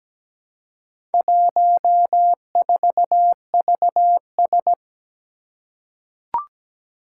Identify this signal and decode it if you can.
Morse code, heard as a keyed tone, about 17 wpm, 700 Hz. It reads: 14VS
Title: Four Challenging Letter-Number Combinations 17wpm